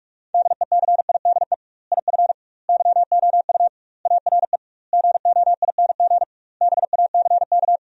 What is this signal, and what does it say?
Morse code, heard as a keyed tone, about 47 wpm, 700 Hz. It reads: DECIDE IF YOU ARE GOING BACK
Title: 2x Word-Spacing - Sentences from Top 500 Words - 45wpm